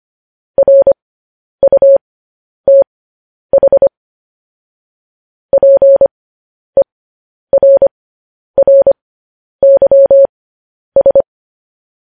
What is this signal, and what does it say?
Morse code, heard as a keyed tone, about 25 wpm, spaced as for 10 wpm, 550 Hz. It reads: RUTH PERRYS
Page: news-headlines-in-morse-code-at-10-wpm